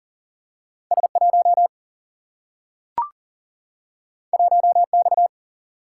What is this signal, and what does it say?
Morse code, heard as a keyed tone, about 40 wpm, 700 Hz. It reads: S1 1X
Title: Single-Letter Single-Number 40wpm